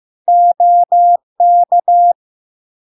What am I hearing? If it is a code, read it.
Morse code, heard as a keyed tone, about 15 wpm, 700 Hz. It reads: OK
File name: LARK-FD-2022-Log-in-CW-15wpm-700-Hz